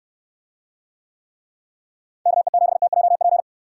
Morse code, heard as a keyed tone, about 69 wpm, 700 Hz. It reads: KE7TJP